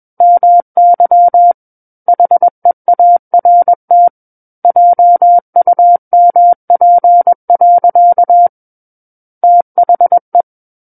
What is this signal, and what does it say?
Morse code, heard as a keyed tone, about 21 wpm, 700 Hz. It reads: MY HEART JUMP. THE
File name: Book One / 12-Chapter 12 - 21 WPM